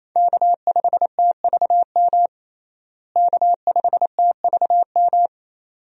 Morse code, heard as a keyed tone, about 28 wpm, 700 Hz. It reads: K5TVM K5TVM
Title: USA Call Signs - 5 Characters - Repeated 2x - 28wpm